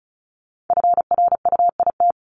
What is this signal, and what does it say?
Morse code, heard as a keyed tone, about 36 wpm, 700 Hz. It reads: FRUIT
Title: Multi-letter Phonemes Repeated 3x 35wpm